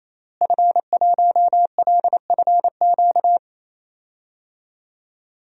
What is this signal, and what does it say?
Morse code, heard as a keyed tone, about 27 wpm, 700 Hz. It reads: F1LFQ
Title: DXCC-100 Call Signs - 5 Characters - Repeated 3x - 28wpm